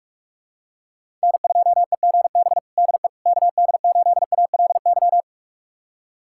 Morse code, heard as a keyed tone, about 45 wpm, 700 Hz. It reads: N2EGB DE KD8ALY
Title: USA Callsign DE Callsign 45wpm